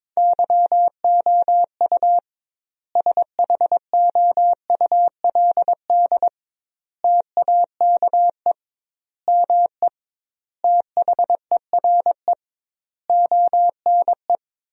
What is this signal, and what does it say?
Morse code, heard as a keyed tone, about 21 wpm, 700 Hz. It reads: YOU SHOULD TAKE ME THERE ONE
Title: 2x Word-Spacing - Sentences from Top 300 Words - 22wpm